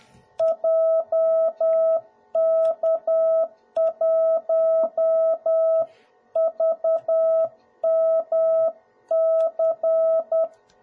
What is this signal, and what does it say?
Morse code, heard as a keyed tone, about 10 wpm, 650 Hz. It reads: JK1VMC
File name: jh8cbh / 2021cw / cw01b